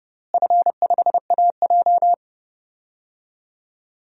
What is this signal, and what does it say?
Morse code, heard as a keyed tone, about 30 wpm, 700 Hz. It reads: F5AJ
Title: DXCC-100 Call Signs - 4 Characters - Repeated 3x - 30wpm